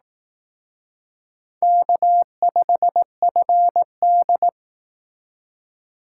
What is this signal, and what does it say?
Morse code, heard as a keyed tone, about 18 wpm, 700 Hz. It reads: K5FD
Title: Speed Racing - USA 1x2 Call Signs 15wpm